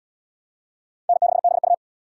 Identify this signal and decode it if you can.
Morse code, heard as a keyed tone, about 69 wpm, 700 Hz. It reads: N6BU